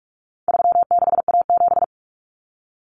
Morse code, heard as a keyed tone, about 44 wpm, 700 Hz. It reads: 36A7
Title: Common Cut Numbers - 4-Digit 45wpm